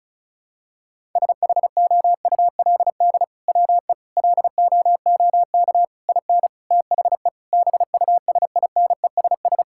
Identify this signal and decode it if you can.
Morse code, heard as a keyed tone, about 36 wpm, 700 Hz. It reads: SHOULD WE LOOK IN THE BUSINESS
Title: Sentences from Top 1600 Words 35wpm